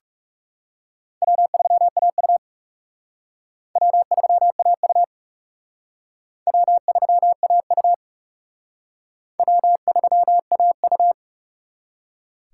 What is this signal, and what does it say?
Morse code, heard as a keyed tone, about 38 wpm, 700 Hz. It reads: W3AU W3AU W3AU W3AU